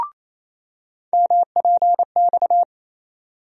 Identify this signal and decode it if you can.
Morse code, heard as a keyed tone, about 28 wpm, 700 Hz. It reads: MPX
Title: USA Call Sign Postfixes - 3 Letters 28wpm